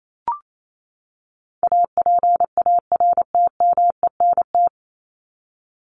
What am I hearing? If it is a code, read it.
Morse code, heard as a keyed tone, about 28 wpm, 700 Hz. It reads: APARTMENT
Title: Top 751-1000 Words 28wpm